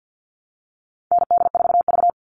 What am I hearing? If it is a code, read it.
Morse code, heard as a keyed tone, about 50 wpm, 700 Hz. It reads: ND4V
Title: All Cut Numbers - 4-Digit 50wpm